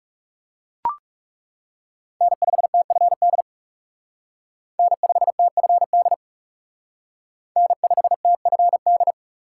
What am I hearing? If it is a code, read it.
Morse code, heard as a keyed tone, about 37 wpm, 700 Hz. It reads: N5TFD N5TFD N5TFD
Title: Speed Racing - USA 1x3 Call Signs 30wpm